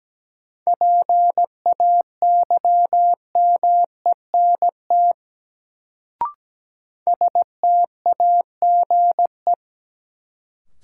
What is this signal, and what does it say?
Morse code, heard as a keyed tone, about 17 wpm, 700 Hz. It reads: PAYMENT STAGE
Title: Sentences from Top 1200 Words - New Word Drill Alt - 17wpm